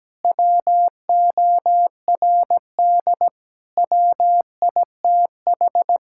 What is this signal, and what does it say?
Morse code, heard as a keyed tone, about 17 wpm, 700 Hz. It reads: WORD WITH
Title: Sentences from Top 1000 Words - Review Part 6 - 17wpm